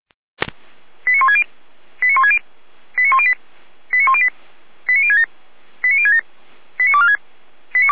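Five-tone selective call (ZVEI), heard as quick runs of five tones, tones of about 70 ms. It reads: 89160 89160 89108 89108 89067 89067 89246
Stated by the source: ZVEI five-tone selective call 89170 received on SDR.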